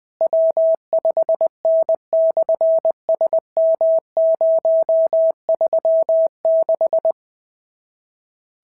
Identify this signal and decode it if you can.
Morse code, heard as a keyed tone, about 20 wpm, 650 Hz. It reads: W5N/SM036
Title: SOTA QSO Practice 20wpm